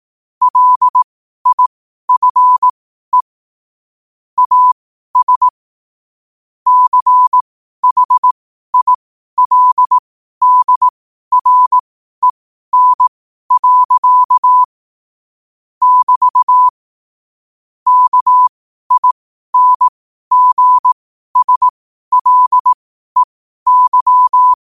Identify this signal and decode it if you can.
Morse code, heard as a keyed tone, about 18 wpm, spaced as for 12 wpm, 1000 Hz. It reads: LIFE AS CHILDREN. <BT> KINGSLEY
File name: qotdfiles / 12wpm